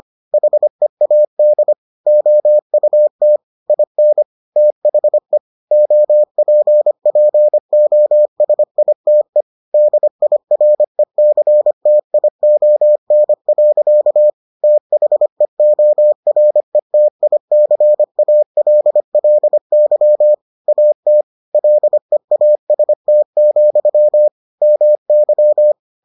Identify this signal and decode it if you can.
Morse code, heard as a keyed tone, about 25 wpm, 600 Hz. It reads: HEAD OUT IN THE OPPOSITE DIRECTION. THEORETICALLY AT LEAST, MY